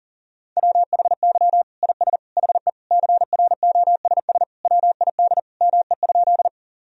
Morse code, heard as a keyed tone, about 37 wpm, 700 Hz. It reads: WHY IS HE CROSS WID ME?
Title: Sentences from Top 300 Words and Abbreviations 40wpm